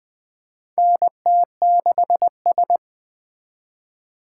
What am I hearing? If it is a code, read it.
Morse code, heard as a keyed tone, about 20 wpm, 700 Hz. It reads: NT6S
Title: USA Call Signs - 4 Characters - 4x Character Spacing - 20wpm